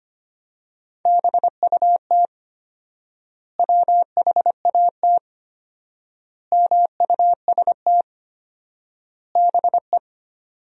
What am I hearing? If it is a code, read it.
Morse code, heard as a keyed tone, about 25 wpm, 700 Hz. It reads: BUT WHAT MUST BE
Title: 4x Word-Spacing - Sentences from Top 500 Words - 25wpm